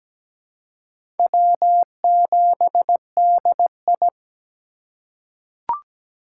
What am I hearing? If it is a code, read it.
Morse code, heard as a keyed tone, about 17 wpm, 700 Hz. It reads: W7DI
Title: USA Call Signs - 4 Characters - 2x Character Spacing - 17wpm